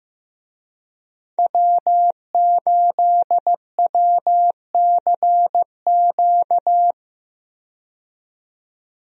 Morse code, heard as a keyed tone, about 15 wpm, 700 Hz. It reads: W8WCQ
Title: USA Call Signs - 5 Characters - 15wpm